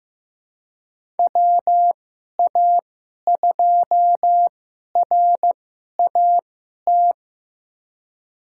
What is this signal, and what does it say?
Morse code, heard as a keyed tone, about 15 wpm, spaced as for 11 wpm, 700 Hz. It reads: WA2RAT
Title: USA Call Signs - 6 Characters - 2x Character Spacing - 15wpm